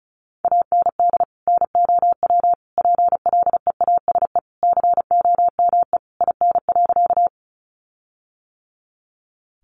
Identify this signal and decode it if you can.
Morse code, heard as a keyed tone, about 34 wpm, 700 Hz. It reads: AND NOW PLEASE COME IN.
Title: Sentences from Top 1500 Words 35wpm